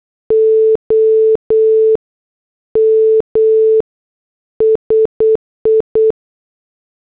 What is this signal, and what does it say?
Morse code, heard as a keyed tone, about 8 wpm, 440 Hz. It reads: OM SI